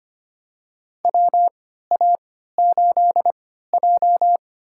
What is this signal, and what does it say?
Morse code, heard as a keyed tone, about 25 wpm, spaced as for 14 wpm, 700 Hz. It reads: WA8J
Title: USA Call Signs - 3x Character Spacing - 25wpm